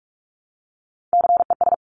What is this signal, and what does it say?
Morse code, heard as a keyed tone, about 46 wpm, 700 Hz. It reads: CES
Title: Top Three Letter Combinations in Words 45wpm